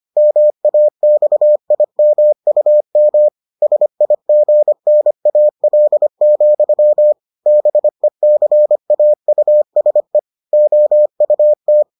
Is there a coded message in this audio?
Yes: Morse code from 0.2 to 11.8 s, heard as a keyed tone, about 25 wpm, 600 Hz. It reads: MAXIMUM SIGNAL, BECAUSE OUT